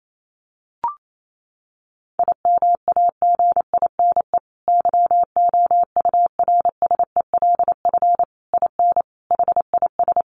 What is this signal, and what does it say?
Morse code, heard as a keyed tone, about 28 wpm, 700 Hz. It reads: IMAGINE YOURSELF IN HIS
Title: Sentences from Top 1400 Words 28wpm